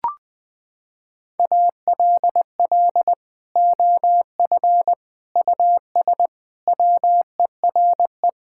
Morse code, heard as a keyed tone, about 20 wpm, 700 Hz. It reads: ALL OF US WERE
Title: Sentences from Top 1000 Words - Review Part 5 - 20wpm